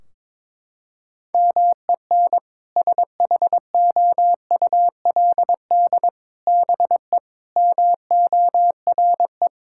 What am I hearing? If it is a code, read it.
Morse code, heard as a keyed tone, about 22 wpm, 700 Hz. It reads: MEN SHOULD BE MORE
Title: Sentences from Top 1600 Words 22wpm